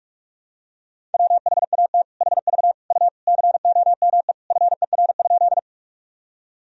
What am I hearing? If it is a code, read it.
Morse code, heard as a keyed tone, about 42 wpm, 700 Hz. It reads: WHAT HV U COME FER?
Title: Sentences from Top 100 Words and Abbreviations 45wpm